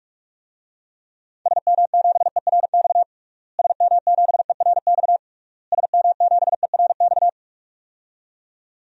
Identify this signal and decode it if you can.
Morse code, heard as a keyed tone, about 46 wpm, 700 Hz. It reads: SM7ERX SM7ERX SM7ERX
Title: DXCC-100 Call Signs - 6 Characters - Repeated 3x - 45wpm